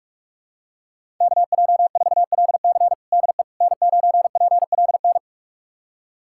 Morse code, heard as a keyed tone, about 41 wpm, 700 Hz. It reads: KJ4LC DE N9PLN
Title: USA Callsign DE Callsign - 5 Characters - 45wpm